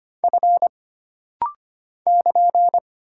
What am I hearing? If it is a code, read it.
Morse code, heard as a keyed tone, about 25 wpm, 700 Hz. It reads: F <KN>
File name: Single Character (Letter:Number:Punctuation:Prosign) - Rapid-Fire - 25wpm